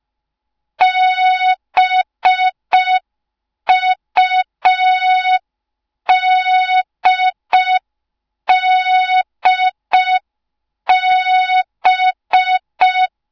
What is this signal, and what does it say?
Morse code, heard as a keyed tone, about 5 wpm, 3700 Hz. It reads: BUDDB